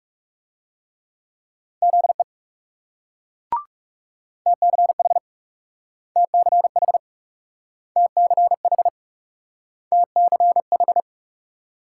Morse code, heard as a keyed tone, about 38 wpm, 700 Hz. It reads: ZE TCH TCH TCH TCH
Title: Speed-Racing Multi-letter Phonemes - 30wpm